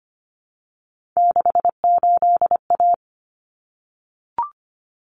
Morse code, heard as a keyed tone, about 25 wpm, 700 Hz. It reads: 68A
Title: Common Cut Numbers - 3-Digit 25wpm